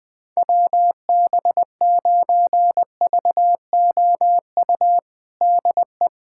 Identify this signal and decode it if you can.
Morse code, heard as a keyed tone, about 20 wpm, 700 Hz. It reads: WB9VOU DE
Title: USA Callsign DE Callsign - 6 Characters - 20wpm